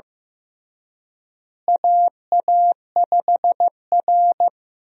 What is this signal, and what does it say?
Morse code, heard as a keyed tone, about 16 wpm, 700 Hz. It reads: AA5R
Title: USA Call Signs - 4 Characters - Repeated 3x - 15wpm